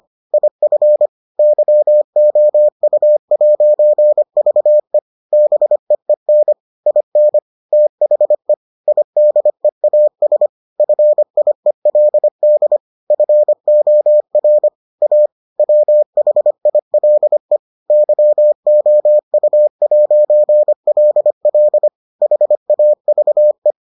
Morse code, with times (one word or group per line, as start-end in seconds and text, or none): 0.3-1.1 IF
1.4-5.0 YOU'VE
5.3-6.5 BEEN
6.9-7.4 IN
7.7-8.5 THE
8.9-10.5 IDEAS
10.8-12.8 FIELD
13.1-14.7 FOR
15.0-15.3 A
15.6-17.6 WHILE
17.9-21.9 YOU'LL
22.2-23.7 HAVE